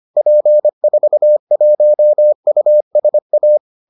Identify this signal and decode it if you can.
Morse code, heard as a keyed tone, about 25 wpm, 600 Hz. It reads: P41USA